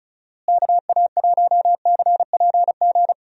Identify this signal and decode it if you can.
Morse code, heard as a keyed tone, about 36 wpm, 700 Hz. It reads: KA1CPG
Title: USA Call Signs - 6 Characters - Repeated 2x - 35wpm